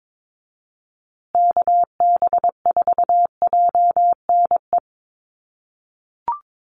Morse code, heard as a keyed tone, about 22 wpm, 700 Hz. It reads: KB4JNE
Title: USA Call Signs Repeated 3x 22wpm